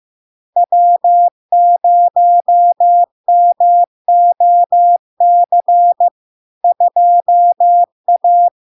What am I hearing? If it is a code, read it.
Morse code, heard as a keyed tone, about 15 wpm, 700 Hz. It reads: W0MOC 2A
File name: LARK-FD-2022-Log-in-CW-15wpm-700-Hz